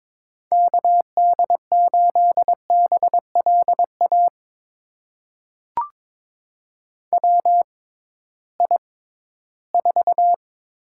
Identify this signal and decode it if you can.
Morse code, heard as a keyed tone, about 22 wpm, spaced as for 14 wpm, 700 Hz. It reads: KD8BLA WI4
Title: USA Call Signs - 6 Characters - 6x Character Spacing - 22wpm